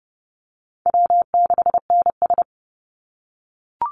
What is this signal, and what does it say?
Morse code, heard as a keyed tone, about 30 wpm, 700 Hz. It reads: W6NS
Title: Easy Call Signs - 30wpm